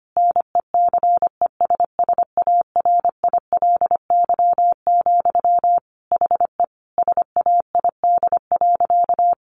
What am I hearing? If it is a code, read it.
Morse code, heard as a keyed tone, about 25 wpm, 700 Hz. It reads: NECESSARILY, HE SAID.
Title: Sentences from Top 1400 Words 25wpm